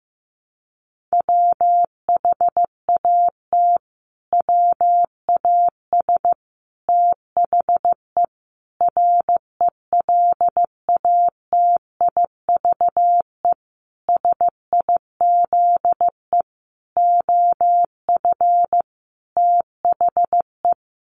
Morse code, heard as a keyed tone, about 15 wpm, 700 Hz. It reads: WHAT WAS THE RELATIVE SIZE OF THE
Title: Sentences from Top 1400 Words 15wpm